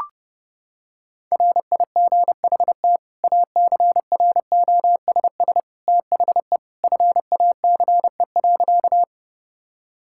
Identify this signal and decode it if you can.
Morse code, heard as a keyed tone, about 30 wpm, 700 Hz. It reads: RIGHT ACROSS THE FACE.
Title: Sentences from Top 1000 Words - Review Part 5 - 30wpm